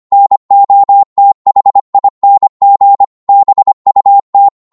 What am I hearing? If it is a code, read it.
Morse code, heard as a keyed tone, about 25 wpm, 800 Hz. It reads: NOTHING BUT
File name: Morse / psg-25wpm